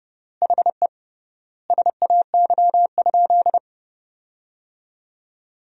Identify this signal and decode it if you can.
Morse code, heard as a keyed tone, about 30 wpm, 700 Hz. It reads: HE SAY?
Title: 3x Word-Spacing - Sentences from Top 200 Words - 30wpm